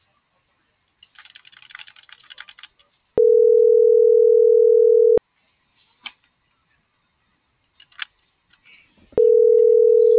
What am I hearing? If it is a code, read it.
Ringback tone (North America), 440 + 480 Hz.